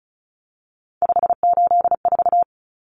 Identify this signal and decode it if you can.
Morse code, heard as a keyed tone, about 34 wpm, 700 Hz. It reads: 584
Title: Common Cut Numbers - 3-Digit 35wpm